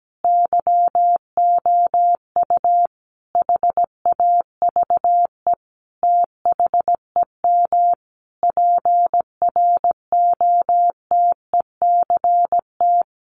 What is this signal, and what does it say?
Morse code, heard as a keyed tone, about 17 wpm, 700 Hz. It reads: YOU HAVE THEM PROTECT
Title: Sentences from Top 900 Words 17wpm